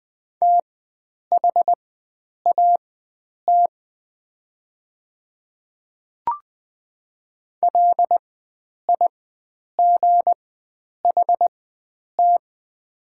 Morse code, heard as a keyed tone, about 20 wpm, spaced as for 9 wpm, 700 Hz. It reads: THAT LIGHT
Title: 4x Character Spacing - Top 300 Words - 20wpm